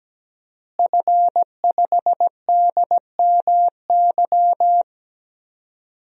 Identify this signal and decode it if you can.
Morse code, heard as a keyed tone, about 17 wpm, 700 Hz. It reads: F5DMY
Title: DXCC-100 Call Signs - 5 Characters - Repeated 2x - 17wpm